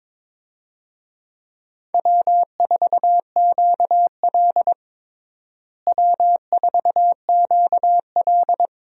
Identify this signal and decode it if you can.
Morse code, heard as a keyed tone, about 22 wpm, 700 Hz. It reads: W4QL W4QL
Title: USA Call Signs - 4 Characters - Repeated 3x - 22wpm